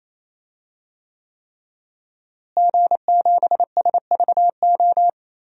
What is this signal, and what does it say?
Morse code, heard as a keyed tone, about 27 wpm, 700 Hz. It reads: G7SVO
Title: DXCC-100 Call Signs - 5 Characters - Repeated 3x - 28wpm